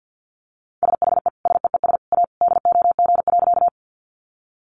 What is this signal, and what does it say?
Morse code, heard as a keyed tone, about 50 wpm, 700 Hz. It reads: SHE SEES A DOG.
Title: Easy Phrases 50wpm